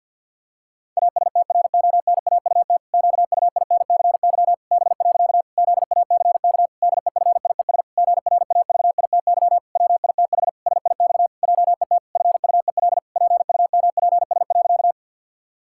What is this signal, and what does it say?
Morse code, heard as a keyed tone, about 50 wpm, 700 Hz. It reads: ASTRONAUT QUINCY B. ZACK DEFIES GRAVITY WITH SIX JET FUEL PUMPS.
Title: Pangram Challenge v2 - 50wpm